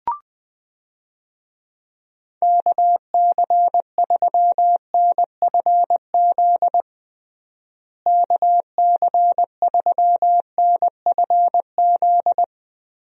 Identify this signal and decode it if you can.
Morse code, heard as a keyed tone, about 20 wpm, 700 Hz. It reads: KC3NFZ KC3NFZ
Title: USA Call Signs - 6 Characters - Repeated 3x - 20wpm